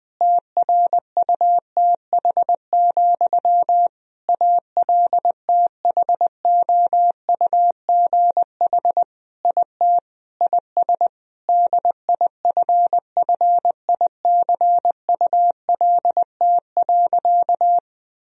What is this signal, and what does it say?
Morse code, heard as a keyed tone, about 20 wpm, 700 Hz. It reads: TRUTH, ALTHOUGH IT IS DIFFICULT.
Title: Sentences from Top 1000 Words - Review Part 3 - 20wpm